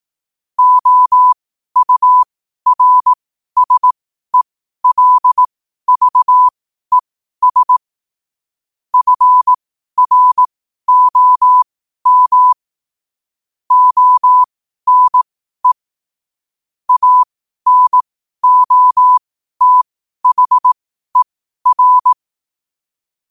Morse code, heard as a keyed tone, about 18 wpm, spaced as for 12 wpm, 1000 Hz. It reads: OURSELVES FROM ONE ANOTHER
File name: qotdfiles / 12wpm